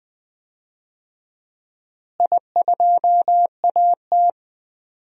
Morse code, heard as a keyed tone, about 20 wpm, 700 Hz. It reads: I2AT
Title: DXCC-100 Call Signs - 4 Characters - Repeated 2x - 20wpm